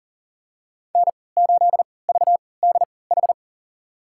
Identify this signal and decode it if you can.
Morse code, heard as a keyed tone, about 39 wpm, spaced as for 22 wpm, 700 Hz. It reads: N8VDH
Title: USA Call Signs - 5 Characters - 3x Character Spacing - 40wpm